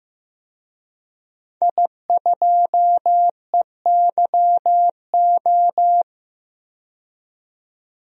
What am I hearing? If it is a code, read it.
Morse code, heard as a keyed tone, about 15 wpm, 700 Hz. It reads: I2EYO
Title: DXCC-100 Call Signs - 5 Characters - 15wpm